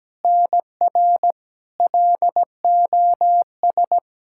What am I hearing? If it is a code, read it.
Morse code, heard as a keyed tone, about 17 wpm, 700 Hz. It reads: NR LOS
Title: K1USN SST QSO Practice 17wpm